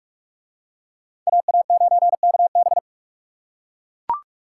Morse code, heard as a keyed tone, about 40 wpm, 700 Hz. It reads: AA9KB